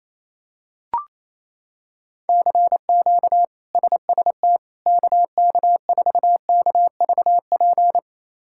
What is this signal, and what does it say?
Morse code, heard as a keyed tone, about 28 wpm, 700 Hz. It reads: CQ SST KK4KVP
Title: K1USN SST QSO Practice - 28wpm